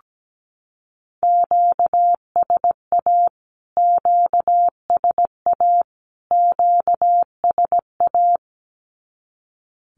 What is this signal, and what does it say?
Morse code, heard as a keyed tone, about 17 wpm, 700 Hz. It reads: QSA QSA QSA
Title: Q-Codes Repeated 3x 17wpm